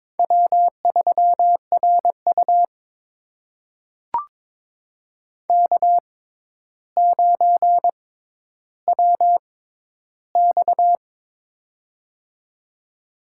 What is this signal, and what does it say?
Morse code, heard as a keyed tone, about 21 wpm, spaced as for 11 wpm, 700 Hz. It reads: W3RU K9WX
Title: USA Call Signs - 4 Characters - 6x Character Spacing - 22wpm